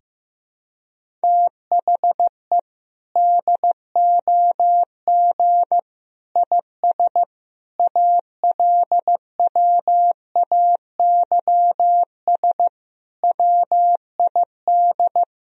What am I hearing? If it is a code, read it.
Morse code, heard as a keyed tone, about 15 wpm, 700 Hz. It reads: THE DOG IS ALWAYS WID
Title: Sentences from Top 400 Words and Abbreviations 15wpm